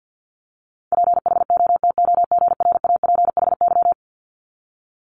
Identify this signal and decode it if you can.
Morse code, heard as a keyed tone, about 50 wpm, 700 Hz. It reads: PHOTOGRAPHY
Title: Multi-letter Phonemes 50wpm